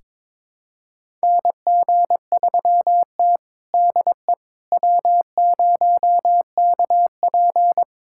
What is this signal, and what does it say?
Morse code, heard as a keyed tone, about 22 wpm, 700 Hz. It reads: NG3T DE W0KP
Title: USA Callsign DE Callsign - 4 Characters - 22wpm